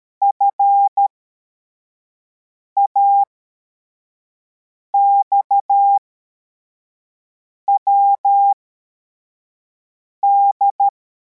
Morse code, heard as a keyed tone, about 13 wpm, spaced as for 4 wpm, 800 Hz. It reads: FAXWD